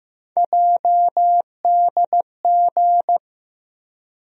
Morse code, heard as a keyed tone, about 15 wpm, 700 Hz. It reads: JDG
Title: USA Call Sign Postfixes - 3 Letters 15wpm